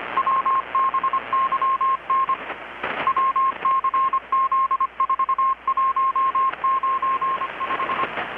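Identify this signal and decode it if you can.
Morse code, heard as a keyed tone, about 25 wpm, 1070 Hz, with radio noise. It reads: WBYN WCZ4195
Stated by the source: Morse from KPH's Night of Nights XX maritime radiotelegraphy event, received in Chiapas, Mexico about 3,800 km away on a Kenwood R-600 with a grounded 10 m long-wire antenna.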